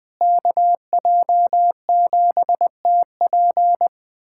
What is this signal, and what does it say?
Morse code, heard as a keyed tone, about 21 wpm, 700 Hz. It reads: KJ7TP
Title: USA Callsign DE Callsign - 5 Characters - 20wpm